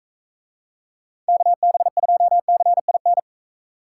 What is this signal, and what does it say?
Morse code, heard as a keyed tone, about 41 wpm, 700 Hz. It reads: KB2CIN